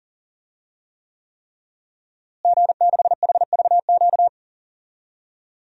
Morse code, heard as a keyed tone, about 40 wpm, 700 Hz. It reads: G6HVQ
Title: DXCC-100 Call Signs - 5 Characters - Repeated 3x - 40wpm